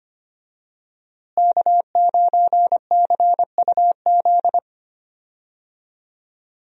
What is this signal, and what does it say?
Morse code, heard as a keyed tone, about 25 wpm, 700 Hz. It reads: K9CUZ
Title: USA Call Signs - 5 Characters - Repeated 2x - 25wpm